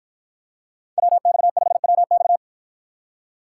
Morse code, heard as a keyed tone, about 50 wpm, 700 Hz. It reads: WX5WX